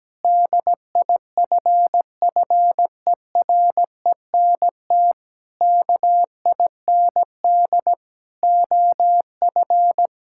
Morse code, heard as a keyed tone, about 17 wpm, 700 Hz. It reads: DIFFERENT KIND OF
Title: Sentences from Top 1000 Words - Review Part 5 - 17wpm